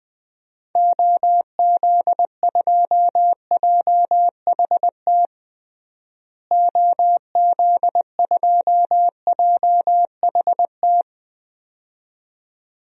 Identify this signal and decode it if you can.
Morse code, heard as a keyed tone, about 20 wpm, 700 Hz. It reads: OZ2JHT OZ2JHT
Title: DXCC-100 Call Signs - 6 Characters - Repeated 2x - 20wpm